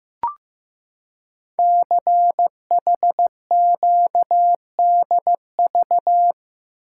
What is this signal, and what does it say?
Morse code, heard as a keyed tone, about 15 wpm, 700 Hz. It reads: CHQDV
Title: Code Groups - 5 Letter:Numbers - 15wpm